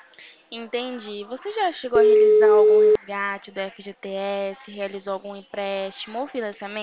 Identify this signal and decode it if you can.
Ringback tone (Europe, 425 Hz), with speech.
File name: out- -2007- - - .193